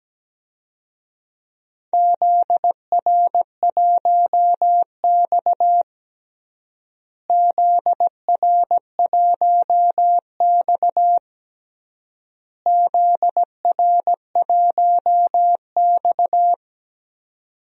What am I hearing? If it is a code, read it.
Morse code, heard as a keyed tone, about 17 wpm, 700 Hz. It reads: ZR1X ZR1X ZR1X
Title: DXCC-100 Call Signs - 4 Characters - Repeated 3x - 17wpm